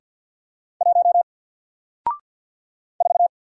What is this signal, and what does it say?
Morse code, heard as a keyed tone, about 48 wpm, 700 Hz. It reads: 1 4
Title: Single Letter-Number Alt 50wpm